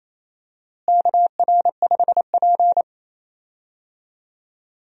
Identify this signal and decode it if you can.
Morse code, heard as a keyed tone, about 28 wpm, 700 Hz. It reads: KR5P
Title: USA Call Signs - 4 Characters - Repeated 3x - 28wpm